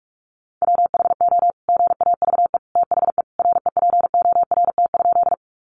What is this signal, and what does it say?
Morse code, heard as a keyed tone, about 44 wpm, 700 Hz. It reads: WHO GAVE THE REPORT?
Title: Sentences from Top 900 Words 45wpm